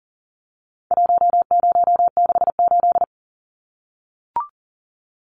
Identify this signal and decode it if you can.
Morse code, heard as a keyed tone, about 40 wpm, 700 Hz. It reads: 1068
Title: Numbers - 4-Digit 40wpm